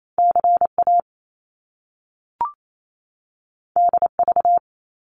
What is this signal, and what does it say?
Morse code, heard as a keyed tone, about 28 wpm, 700 Hz. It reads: C A D V
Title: Top Two Letter Combinations in Words 28wpm